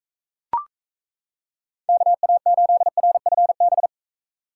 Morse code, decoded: KA8RFB

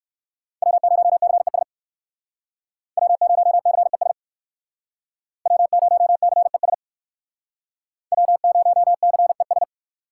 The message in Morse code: W0CES W0CES W0CES W0CES

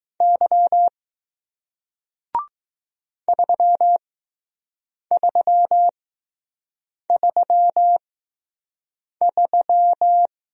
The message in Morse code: Y 3333

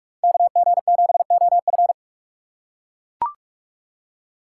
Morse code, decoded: KG7OF

